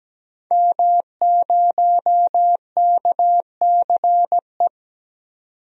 Morse code, heard as a keyed tone, about 17 wpm, 700 Hz. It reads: M0KCE
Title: DXCC-100 Call Signs - 5 Characters - 17wpm